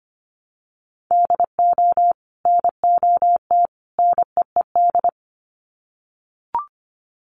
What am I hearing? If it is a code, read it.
Morse code, heard as a keyed tone, about 25 wpm, 700 Hz. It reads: DO NOT NEED
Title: Sets of 3 Words 25wpm